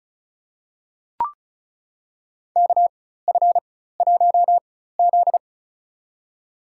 Morse code, heard as a keyed tone, about 36 wpm, spaced as for 16 wpm, 700 Hz. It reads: KF1Z
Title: USA Call Signs - 4 Characters - 4x Character Spacing - 35wpm